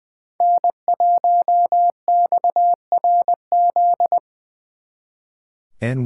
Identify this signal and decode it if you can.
Morse code, heard as a keyed tone, about 23 wpm, 700 Hz. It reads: N1XRZ
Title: Speed Racing - USA 1x3 Call Signs 20wpm